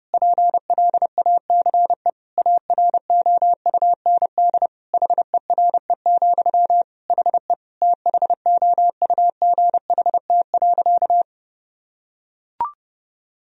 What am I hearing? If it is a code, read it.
Morse code, heard as a keyed tone, about 30 wpm, 700 Hz. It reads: PLACE AROUND HERE, HE THOUGHT.
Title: Sentences from Top 1000 Words - Review Part 3 - 30wpm